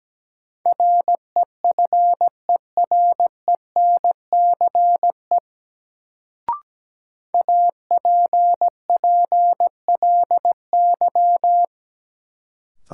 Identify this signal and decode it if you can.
Morse code, heard as a keyed tone, about 17 wpm, 700 Hz. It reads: REFERENCE APPLY
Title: Sentences from Top 1200 Words - New Word Drill - 17wpm